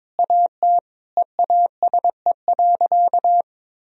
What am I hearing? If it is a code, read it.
Morse code, heard as a keyed tone, about 22 wpm, 700 Hz. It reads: AT EASE.
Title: Sentences from Top 1000 Words - Review Part 1 - 22wpm